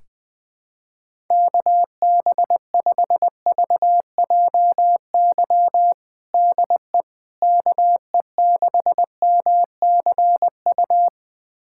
Morse code, heard as a keyed tone, about 20 wpm, 700 Hz. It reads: KB5VJY DE KE6MCU
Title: USA Callsign DE Callsign 20wpm